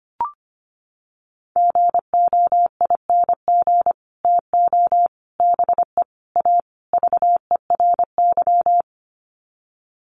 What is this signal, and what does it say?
Morse code, heard as a keyed tone, about 25 wpm, 700 Hz. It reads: GOING TO BE A VERY
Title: Sets of 5 Words Encore 25wpm